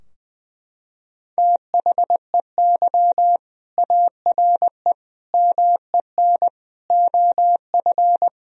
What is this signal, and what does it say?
Morse code, heard as a keyed tone, about 21 wpm, 700 Hz. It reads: THEY ARE MEN OF
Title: Sentences from Top 1000 Words - Review Part 2 - 20wpm